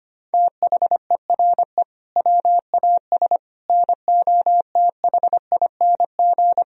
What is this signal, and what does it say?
Morse code, heard as a keyed tone, about 25 wpm, 700 Hz. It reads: THERE WAS NOTHING
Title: Sentences from Top 1000 Words - Review Part 1 - 25wpm